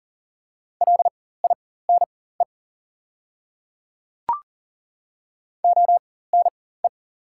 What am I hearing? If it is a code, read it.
Morse code, heard as a keyed tone, about 40 wpm, spaced as for 19 wpm, 690 Hz. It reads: LINE ONE
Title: 4x Character Spacing - Top 200 Words - 40wpm